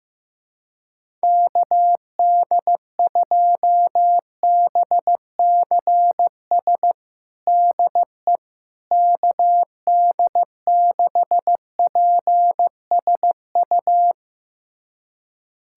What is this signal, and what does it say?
Morse code, heard as a keyed tone, about 16 wpm, 700 Hz. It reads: KD2BCS DE KD6PSU
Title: USA Callsign DE Callsign 15wpm